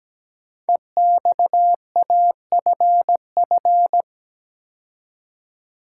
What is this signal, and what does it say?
Morse code, heard as a keyed tone, about 17 wpm, 700 Hz. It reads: EXAFF
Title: Code Groups - 5 Letters - 17wpm